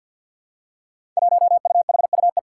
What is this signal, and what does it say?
Morse code, heard as a keyed tone, about 51 wpm, 700 Hz. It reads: 1UHUE
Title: Five Challenging Letter-Number Combinations 50wpm